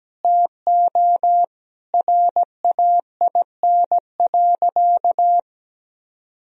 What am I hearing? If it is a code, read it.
Morse code, heard as a keyed tone, about 17 wpm, 700 Hz. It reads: TO RAIN.
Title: Sentences from Top 1000 Words - Review Part 1 - 17wpm